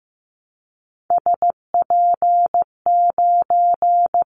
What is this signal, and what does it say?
Morse code, heard as a keyed tone, about 15 wpm, 700 Hz. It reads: SP9
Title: Callsign Prefixes for Top 25 DXCC - Repeated 3x - 15wpm